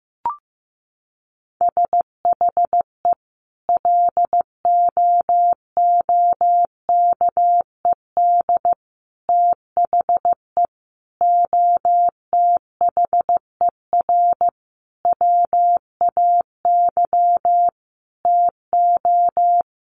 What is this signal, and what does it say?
Morse code, heard as a keyed tone, about 15 wpm, 700 Hz. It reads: SHE LOOKED THE OTHER WAY TO
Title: Sentences from Top 800 Words 15wpm